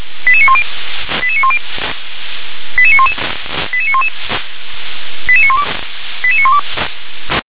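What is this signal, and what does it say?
Five-tone selective call (ZVEI), heard as quick runs of five tones, tones of about 70 ms. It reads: 88910 88910 88911 88911 88912 88912
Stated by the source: ZVEI five-tone call 88922 received by SDR.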